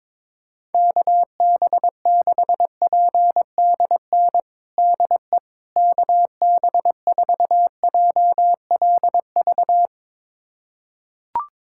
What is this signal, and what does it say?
Morse code, heard as a keyed tone, about 22 wpm, 700 Hz. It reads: KB6PDN DE KB4JLV
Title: USA Callsign DE Callsign - 6 Characters - 22wpm